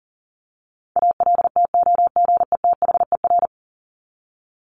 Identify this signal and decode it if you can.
Morse code, heard as a keyed tone, about 40 wpm, 700 Hz. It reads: ALTOGETHER
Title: Top 1501-1750 Words 40wpm